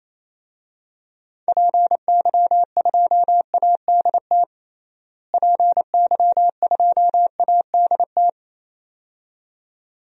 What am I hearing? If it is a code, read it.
Morse code, heard as a keyed tone, about 28 wpm, 700 Hz. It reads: PY2ADT PY2ADT
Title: DXCC-100 Call Signs - 6 Characters - Repeated 2x - 28wpm